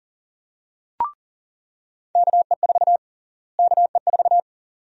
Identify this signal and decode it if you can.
Morse code, heard as a keyed tone, about 37 wpm, 700 Hz. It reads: KE4 KE4
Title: USA Call Sign Prefixes - Repeated 2x - 40wpm